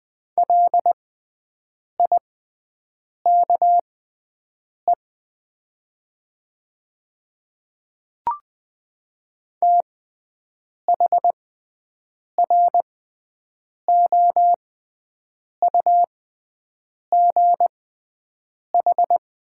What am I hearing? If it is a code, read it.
Morse code, heard as a keyed tone, about 20 wpm, spaced as for 7 wpm, 700 Hz. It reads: LIKE THROUGH
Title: 6x Character Spacing - Top 200 Words - 20wpm